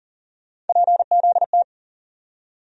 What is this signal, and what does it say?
Morse code, heard as a keyed tone, about 39 wpm, 700 Hz. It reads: PZT
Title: USA Call Sign Postfixes - 3 Letters 40wpm